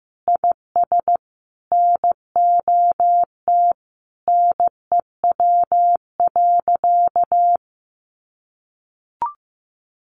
Morse code, heard as a keyed tone, about 15 wpm, 700 Hz. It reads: IS NOT NEW.
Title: Sentences from Top 1100 Words 15wpm